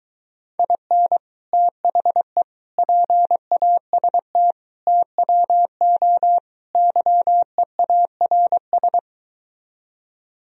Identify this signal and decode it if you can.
Morse code, heard as a keyed tone, about 23 wpm, 700 Hz. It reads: IN THE PAST TWO YEARS